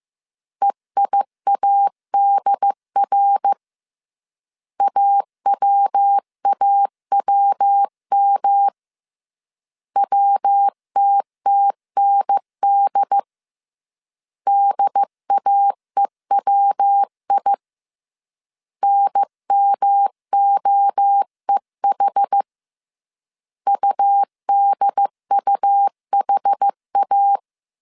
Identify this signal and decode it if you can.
Morse code, heard as a keyed tone, about 15 wpm, 800 Hz. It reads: EIADR AWAWM WTTND DAEWI NMOEH UDUHA
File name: CW / DET / 1 3 1500C S C 14wpm 2458